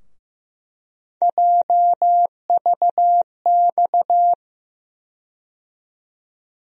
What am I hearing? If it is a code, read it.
Morse code, heard as a keyed tone, about 15 wpm, 700 Hz. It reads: JVX
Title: USA Call Sign Postfixes - 3 Letters 15wpm